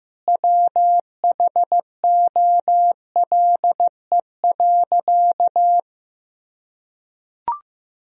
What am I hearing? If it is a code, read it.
Morse code, heard as a keyed tone, about 15 wpm, 700 Hz. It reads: WHOLE.
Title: Sentences from Top 1000 Words - Review Part 1 - 15wpm